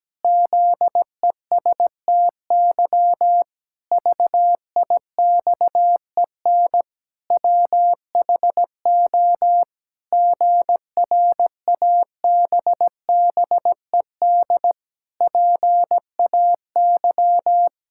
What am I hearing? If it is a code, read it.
Morse code, heard as a keyed tone, about 17 wpm, 700 Hz. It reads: ZESTY VIXEN WHO GRABBED PAY
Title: Pangram Challenge v2 - 17wpm